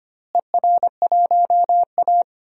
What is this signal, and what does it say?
Morse code, heard as a keyed tone, about 25 wpm, 700 Hz. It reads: ER1A
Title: DXCC-100 Call Signs - 4 Characters - Repeated 3x - 25wpm